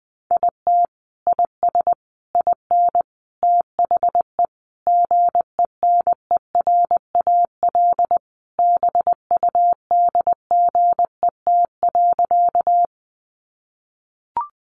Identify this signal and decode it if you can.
Morse code, heard as a keyed tone, about 20 wpm, 700 Hz. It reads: IT IS IN THE GENERAL BUDGET.
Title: Sentences from Top 1200 Words 20wpm